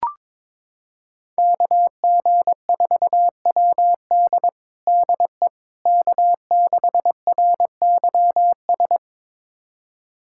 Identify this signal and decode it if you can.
Morse code, heard as a keyed tone, about 21 wpm, 700 Hz. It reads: KG4WD DE K6RYS
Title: USA Callsign DE Callsign - 5 Characters - 22wpm